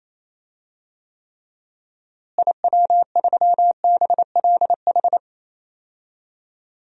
Morse code, heard as a keyed tone, about 28 wpm, 700 Hz. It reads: IW3BLH
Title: DXCC-100 Call Signs - 6 Characters - Repeated 3x - 28wpm